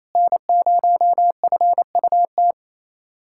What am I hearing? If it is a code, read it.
Morse code, heard as a keyed tone, about 28 wpm, 700 Hz. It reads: N0FUT
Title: USA Callsign DE Callsign - 5 Characters - 28wpm